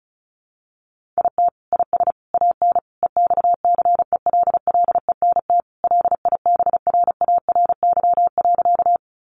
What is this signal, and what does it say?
Morse code, heard as a keyed tone, about 34 wpm, 700 Hz. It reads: IT IS AN EXCELLENT LIBRARY.
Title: Sentences from Top 1000 Words 35wpm